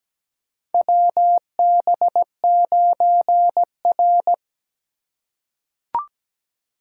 Morse code, heard as a keyed tone, about 17 wpm, 700 Hz. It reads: WB9R
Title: USA Call Signs - 4 Characters - 3x Character Spacing - 17wpm